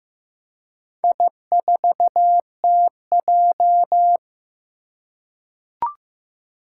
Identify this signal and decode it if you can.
Morse code, heard as a keyed tone, about 15 wpm, 700 Hz. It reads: I4TJ
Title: Four Challenging Letter-Number Combinations 15wpm